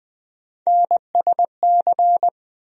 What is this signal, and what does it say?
Morse code, heard as a keyed tone, about 20 wpm, 700 Hz. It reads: NSC
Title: USA Call Sign Postfixes - 3 Letters 20wpm